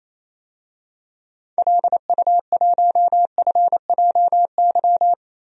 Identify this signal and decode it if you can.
Morse code, heard as a keyed tone, about 28 wpm, 700 Hz. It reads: LU1FJY
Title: DXCC-100 Call Signs - 6 Characters - 28wpm